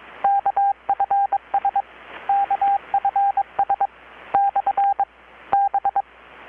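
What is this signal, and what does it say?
Morse code, heard as a keyed tone, about 22 wpm, 800 Hz, with radio noise. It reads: KFS KFS / B
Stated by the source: Morse from KPH's Night of Nights XX maritime radiotelegraphy event, received in Chiapas, Mexico about 3,800 km away on a Kenwood R-600 with a grounded 10 m long-wire antenna.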